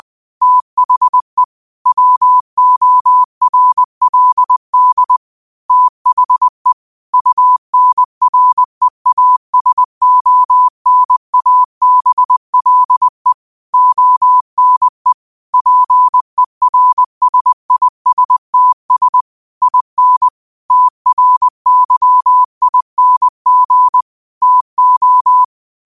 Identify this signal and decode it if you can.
Morse code, heard as a keyed tone, about 20 wpm, 1000 Hz. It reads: THE WORLD THE UNREASONABLE ONE PERSISTS IN TRYING TO